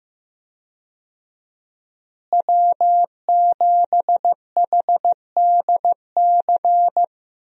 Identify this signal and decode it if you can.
Morse code, heard as a keyed tone, about 15 wpm, 700 Hz. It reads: W7HDC
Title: USA Call Signs - 5 Characters - Repeated 2x - 15wpm